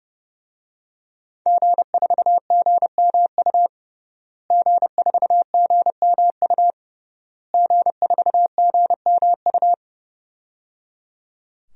Morse code, heard as a keyed tone, about 30 wpm, 700 Hz. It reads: G4GMU G4GMU G4GMU
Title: DXCC-100 Call Signs - 5 Characters - Repeated 3x - 30wpm